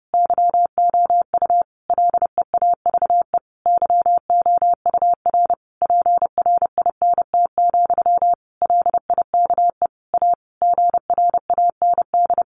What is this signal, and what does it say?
Morse code, heard as a keyed tone, about 30 wpm, 700 Hz. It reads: YOU LEAVE YOUR PRINT, LIKE A GRAND
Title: A Sound Of Thunder 30wpm